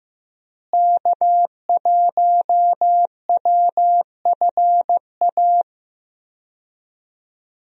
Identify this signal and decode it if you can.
Morse code, heard as a keyed tone, about 15 wpm, 700 Hz. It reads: K1WFA
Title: USA Call Signs - 5 Characters - Repeated 2x - 15wpm